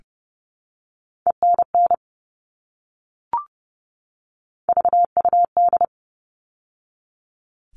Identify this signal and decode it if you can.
Morse code, heard as a keyed tone, about 30 wpm, 700 Hz. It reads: ENN VUD
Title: All Cut Numbers - 3-Digit 30wpm